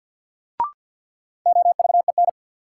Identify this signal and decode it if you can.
Morse code, heard as a keyed tone, about 51 wpm, 700 Hz. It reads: OVEN
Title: Sentences from Top 1000 Words - Review Part 4 - New Word Drill - 50wpm